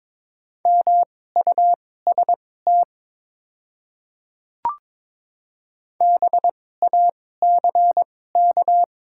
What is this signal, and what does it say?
Morse code, heard as a keyed tone, about 22 wpm, spaced as for 16 wpm, 700 Hz. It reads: MUST BACK
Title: 2x Character Spacing - Top 200 Words - 22wpm